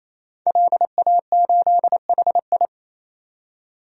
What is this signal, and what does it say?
Morse code, heard as a keyed tone, about 28 wpm, 700 Hz. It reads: LA8HI
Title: DXCC-100 Call Signs - 5 Characters - 28wpm